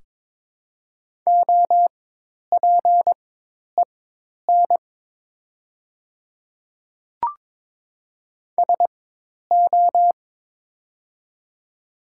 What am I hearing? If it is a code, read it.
Morse code, heard as a keyed tone, about 22 wpm, spaced as for 10 wpm, 700 Hz. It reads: OPEN SO
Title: 4x Character Spacing - Top 300 Words - 22wpm